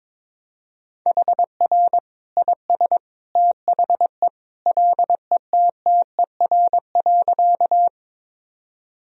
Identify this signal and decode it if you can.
Morse code, heard as a keyed tone, about 22 wpm, 700 Hz. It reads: HR IS THE LETTER.
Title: Sentences from Top 300 Words and Abbreviations 22wpm